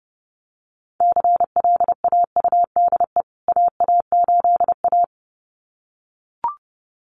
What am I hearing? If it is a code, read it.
Morse code, heard as a keyed tone, about 30 wpm, 700 Hz. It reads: CLAUDE AA8A
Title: CWT Exchange 30wpm